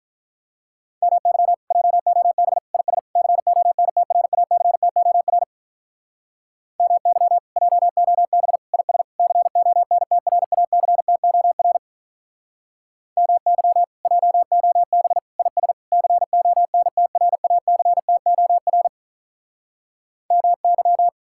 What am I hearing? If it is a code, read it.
Morse code, heard as a keyed tone, about 45 wpm, 700 Hz. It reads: MY JOB IS CONTRACTOR MY JOB IS CONTRACTOR MY JOB IS CONTRACTOR MY